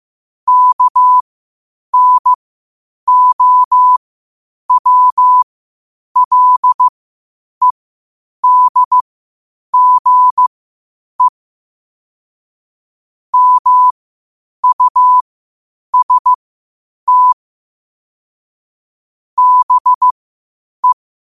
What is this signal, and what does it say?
Morse code, heard as a keyed tone, about 15 wpm, spaced as for 8 wpm, 1000 Hz. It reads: KNOWLEDGE MUST BE
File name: qotdfiles / 8wpm